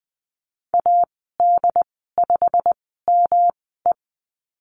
Morse code, heard as a keyed tone, about 20 wpm, spaced as for 14 wpm, 700 Hz. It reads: AD5ME
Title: 147 Lesson 24g - Letter y - Call Signs 20wpm